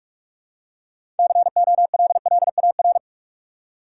Morse code, heard as a keyed tone, about 46 wpm, 700 Hz. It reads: COLLAR
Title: Six Letter Words - 45wpm